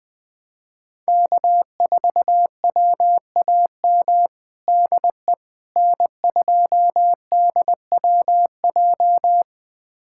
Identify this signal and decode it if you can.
Morse code, heard as a keyed tone, about 21 wpm, 700 Hz. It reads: K4WAM DE N2DWJ
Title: USA Callsign DE Callsign - 5 Characters - 20wpm